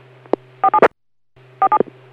DTMF, heard as telephone touch tones, tones of about 70 ms. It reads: 47 47